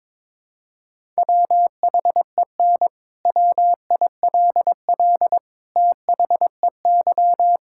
Morse code, heard as a keyed tone, about 22 wpm, 700 Hz. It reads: WHEN WILL THEY
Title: Sentences from Top 1000 Words - Review Part 6 - 22wpm